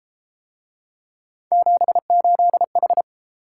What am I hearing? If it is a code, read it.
Morse code, heard as a keyed tone, about 33 wpm, 700 Hz. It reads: 78H